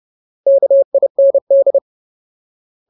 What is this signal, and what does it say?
Morse code, heard as a keyed tone, about 30 wpm, 550 Hz. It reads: KIND
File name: Ultimate Word List / 30WPM Ultimate